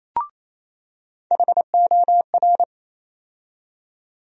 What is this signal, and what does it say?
Morse code, heard as a keyed tone, about 27 wpm, 700 Hz. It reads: HOR
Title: USA Call Sign Postfixes - 3 Letters 28wpm